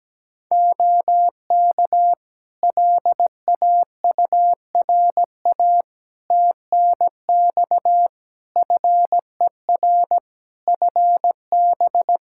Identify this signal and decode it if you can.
Morse code, heard as a keyed tone, about 17 wpm, 700 Hz. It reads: OK LAURA TNX FER FB
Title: K1USN SST QSO Practice 17wpm